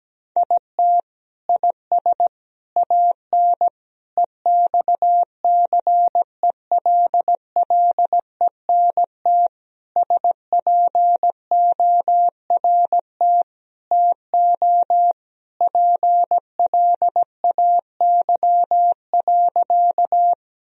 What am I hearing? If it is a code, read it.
Morse code, heard as a keyed tone, about 17 wpm, 700 Hz. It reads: IT IS AN EXCELLENT SPORT TO PLAY.
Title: Sentences from Top 1000 Words - Review Part 4 - 17wpm